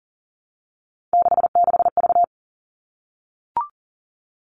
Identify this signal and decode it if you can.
Morse code, heard as a keyed tone, about 40 wpm, 700 Hz. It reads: 66V
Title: All Cut Numbers - 3-Digit 40wpm